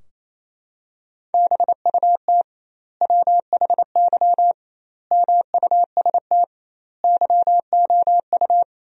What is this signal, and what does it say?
Morse code, heard as a keyed tone, about 28 wpm, 700 Hz. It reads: BUT WHY MUST YOU
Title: 2x Word-Spacing - Sentences from Top 400 Words - 28wpm